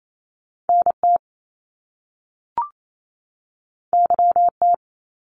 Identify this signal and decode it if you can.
Morse code, heard as a keyed tone, about 28 wpm, 700 Hz. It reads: NT YT
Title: Canadian Provinces and Territories 28wpm